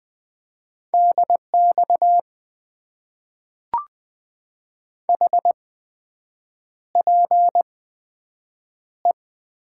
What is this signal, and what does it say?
Morse code, heard as a keyed tone, about 20 wpm, spaced as for 7 wpm, 700 Hz. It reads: DX HPE
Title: Top 90 QSO Elements - 8x Character Spacing - 20wpm